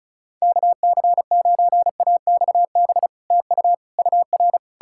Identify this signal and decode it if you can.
Morse code, heard as a keyed tone, about 46 wpm, 700 Hz. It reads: KC9AXB TU UR